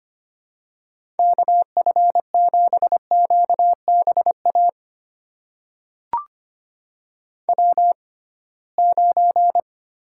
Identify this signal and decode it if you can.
Morse code, heard as a keyed tone, about 25 wpm, spaced as for 19 wpm, 700 Hz. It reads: KF7QBA W9
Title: USA Call Signs - 6x Character Spacing - 25wpm